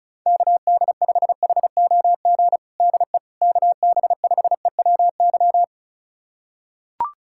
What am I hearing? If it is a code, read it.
Morse code, heard as a keyed tone, about 34 wpm, 700 Hz. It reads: KD5HOG DE KB5EWY
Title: USA Callsign DE Callsign - 6 Characters - 35wpm